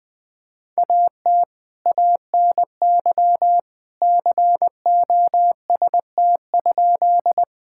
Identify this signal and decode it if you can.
Morse code, heard as a keyed tone, about 20 wpm, 700 Hz. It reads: AT ANY COST?
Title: Sentences from Top 1000 Words - Review Part 5 - 20wpm